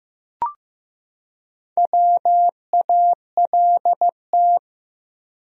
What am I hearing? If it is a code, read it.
Morse code, heard as a keyed tone, about 15 wpm, 700 Hz. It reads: WALT
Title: USA Common Names - Max 5 Letters - 15wpm